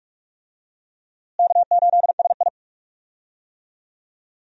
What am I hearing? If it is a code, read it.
Morse code, heard as a keyed tone, about 41 wpm, 700 Hz. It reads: K8SI
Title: USA Call Signs - 4 Characters - Repeated 3x - 45wpm